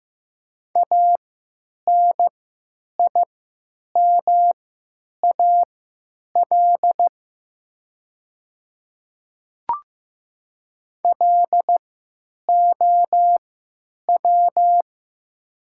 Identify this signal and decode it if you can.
Morse code, heard as a keyed tone, about 15 wpm, spaced as for 9 wpm, 700 Hz. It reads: ANIMAL LOW
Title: 3x Character Spacing - Top 200 Words - 15wpm